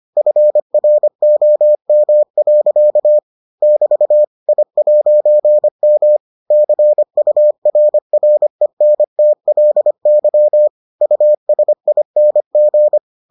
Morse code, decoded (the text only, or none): FROM. <BT> I'M CURRENTLY USING